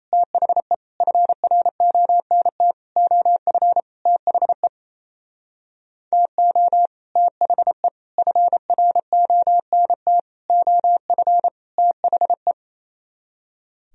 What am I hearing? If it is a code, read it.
Morse code, heard as a keyed tone, about 30 wpm, 700 Hz. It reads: THE FRONT OF THE TO THE FRONT OF THE
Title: Speed-Racing - Sets of 5 Words 28wpm